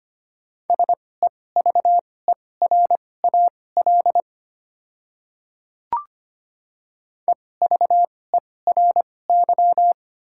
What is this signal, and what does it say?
Morse code, heard as a keyed tone, about 25 wpm, spaced as for 18 wpm, 690 Hz. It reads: SEVERAL EVERY
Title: 2x Character Spacing - Top 500 Words - 25wpm